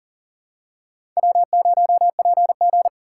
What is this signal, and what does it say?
Morse code, heard as a keyed tone, about 39 wpm, 700 Hz. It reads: W0PG
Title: USA Call Signs - 3x Character Spacing - 40wpm